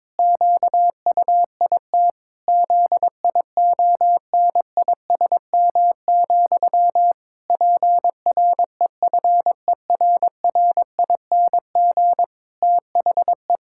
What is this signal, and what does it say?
Morse code, heard as a keyed tone, about 22 wpm, 700 Hz. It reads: QUIT ZIONISM, PREFERRING THE
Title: Pangram Challenge v2 - 22wpm